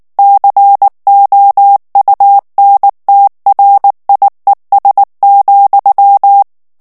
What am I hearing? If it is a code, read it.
Morse code, heard as a keyed tone, about 19 wpm, 800 Hz. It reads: COUNTRIES,
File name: ANGLAIS texte clair 2 vitesse 18 mots minute